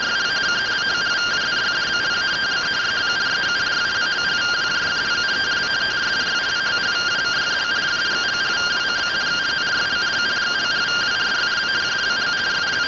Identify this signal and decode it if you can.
Two-tone teleprinter signal, about 170 Hz shift.